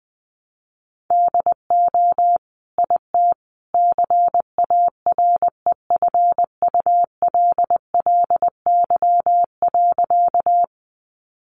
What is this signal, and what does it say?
Morse code, heard as a keyed tone, about 20 wpm, 700 Hz. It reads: DO IT CAREFULLY.
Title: Sentences from Top 1300 Words 20wpm